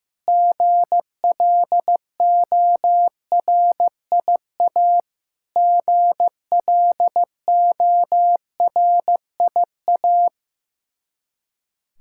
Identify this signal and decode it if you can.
Morse code, heard as a keyed tone, about 16 wpm, 700 Hz. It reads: GLORIA GLORIA
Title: K1USN SST QSO Practice 15wpm